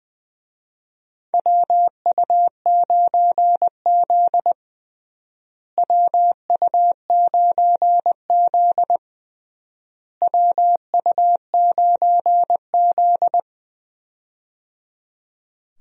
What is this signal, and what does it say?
Morse code, heard as a keyed tone, about 20 wpm, 700 Hz. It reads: WU9Z WU9Z WU9Z
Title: USA Call Signs - 4 Characters - Repeated 3x - 20wpm